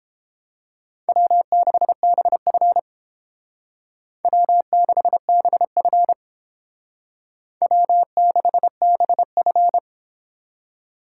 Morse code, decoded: W6BF W6BF W6BF

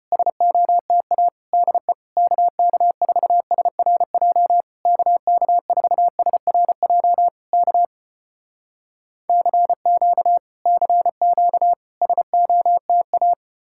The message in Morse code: SOTA DE KK4SRJ KK4SRJ K CQ CQ SOTA

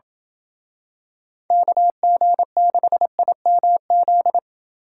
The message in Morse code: KG6IMZ

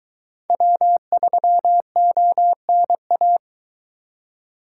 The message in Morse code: W3ONA

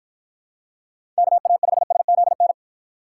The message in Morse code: KA5SZN

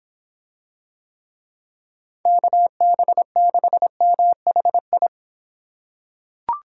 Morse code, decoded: KB6MHI

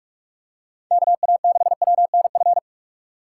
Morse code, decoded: KA6WNF

